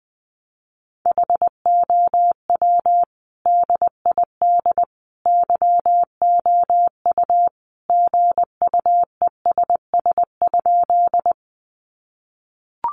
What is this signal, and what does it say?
Morse code, heard as a keyed tone, about 20 wpm, 700 Hz. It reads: HOW DID YOU GUESS?
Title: Sentences from Top 1500 Words 20wpm